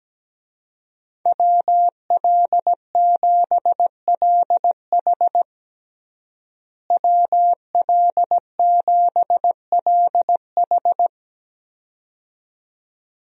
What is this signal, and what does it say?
Morse code, heard as a keyed tone, about 17 wpm, 700 Hz. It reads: WL7LH WL7LH
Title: USA Call Signs - 5 Characters - Repeated 2x - 17wpm